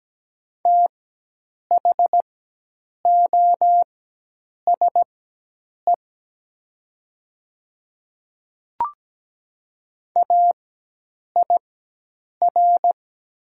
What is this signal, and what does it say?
Morse code, heard as a keyed tone, about 17 wpm, spaced as for 8 wpm, 700 Hz. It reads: THOSE AIR
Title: 4x Character Spacing - Top 500 Words - 17wpm